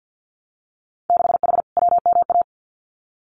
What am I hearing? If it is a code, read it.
Morse code, heard as a keyed tone, about 50 wpm, 700 Hz. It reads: 6H WMA
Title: Winter Field Day Exchanges Encore 50wpm